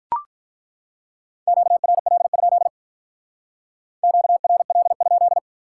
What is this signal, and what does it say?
Morse code, heard as a keyed tone, about 43 wpm, 700 Hz. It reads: QRL? QRL?
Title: Speed-Racing Common QSO Phrases Encore - 35wpm